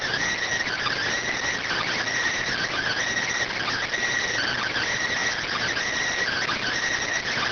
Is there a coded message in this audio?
Yes: Emergency Alert System (SAME) tones (header bursts) from 0.2 to 6.6 s, heard as harsh buzzing bursts.